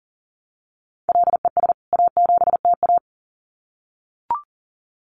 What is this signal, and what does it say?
Morse code, heard as a keyed tone, about 39 wpm, 700 Hz. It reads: LES A7TA
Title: CWT Exchange 40wpm